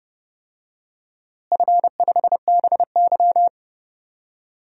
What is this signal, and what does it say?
Morse code, heard as a keyed tone, about 30 wpm, 700 Hz. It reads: F5BY
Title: DXCC-100 Call Signs - 4 Characters - Repeated 3x - 30wpm